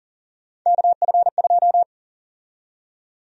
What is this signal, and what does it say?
Morse code, heard as a keyed tone, about 38 wpm, 700 Hz. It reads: KF2
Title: USA Call Sign Prefixes - Repeated 2x - 40wpm